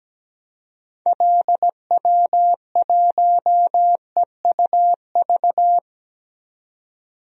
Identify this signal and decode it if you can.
Morse code, heard as a keyed tone, about 17 wpm, 700 Hz. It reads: LW1EUV
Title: DXCC-100 Call Signs - 6 Characters - 17wpm